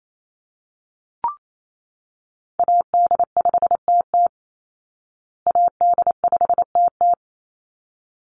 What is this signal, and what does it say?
Morse code, heard as a keyed tone, about 28 wpm, 700 Hz. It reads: AD5TT AD5TT
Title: Call Signs Repeated 3x - 28wpm